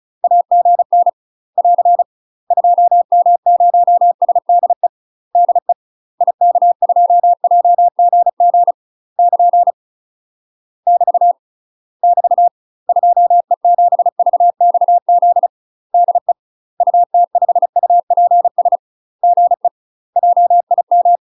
Morse code, heard as a keyed tone, about 36 wpm, 700 Hz. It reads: AGN <AR> 2M0SDE DE IK2JGG <KN> <BT> <BT> 2E7VXZ DE UT5UPS GE JIM
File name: UK MASTER QSOs TRAINING / 35WPM / UK MASTER QSOs 0023